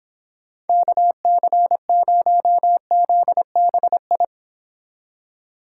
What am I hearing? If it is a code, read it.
Morse code, heard as a keyed tone, about 26 wpm, 700 Hz. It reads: KC0ZBI